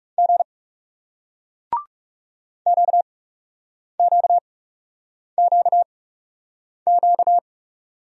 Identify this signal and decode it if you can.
Morse code, heard as a keyed tone, about 40 wpm, spaced as for 8 wpm, 700 Hz. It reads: G QQQQ